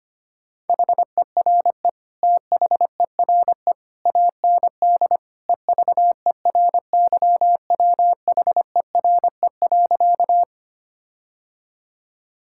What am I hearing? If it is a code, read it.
Morse code, heard as a keyed tone, about 25 wpm, 700 Hz. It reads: HERE THERE AND EVERYWHERE.
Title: Sentences from Top 1600 Words 25wpm